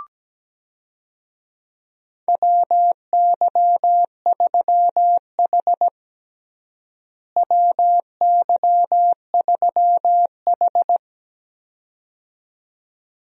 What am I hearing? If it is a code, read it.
Morse code, heard as a keyed tone, about 17 wpm, 700 Hz. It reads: WY3H WY3H
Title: USA Call Signs - 4 Characters - Repeated 2x - 17wpm